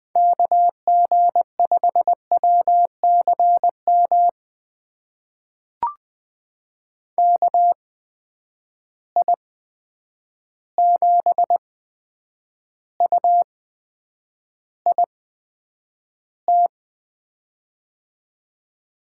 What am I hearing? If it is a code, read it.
Morse code, heard as a keyed tone, about 20 wpm, spaced as for 9 wpm, 700 Hz. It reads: KG5WCM KI7UIT
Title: USA Call Signs - 6 Characters - 8x Character Spacing - 20wpm